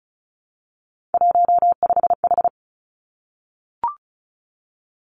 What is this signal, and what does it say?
Morse code, heard as a keyed tone, about 36 wpm, 700 Hz. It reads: 15H
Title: Three Challenging Letter-Number Combinations 35wpm